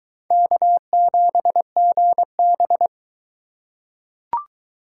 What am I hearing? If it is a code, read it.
Morse code, heard as a keyed tone, about 20 wpm, 700 Hz. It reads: K7GB